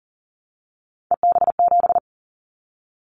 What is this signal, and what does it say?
Morse code, heard as a keyed tone, about 41 wpm, 700 Hz. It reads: EB7
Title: Callsign Prefixes for Top 25 DXCC - 40wpm